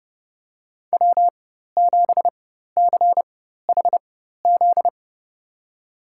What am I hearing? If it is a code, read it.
Morse code, heard as a keyed tone, about 30 wpm, spaced as for 14 wpm, 700 Hz. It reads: W7CHZ
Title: USA Call Signs - 4x Character Spacing - 30wpm